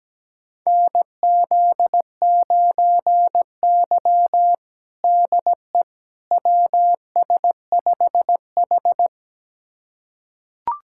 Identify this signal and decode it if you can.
Morse code, heard as a keyed tone, about 17 wpm, 700 Hz. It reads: NZ9Y DE WS5H
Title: USA Callsign DE Callsign - 4 Characters - 17wpm